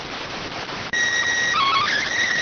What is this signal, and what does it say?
SSTV (slow-scan television), PD 290.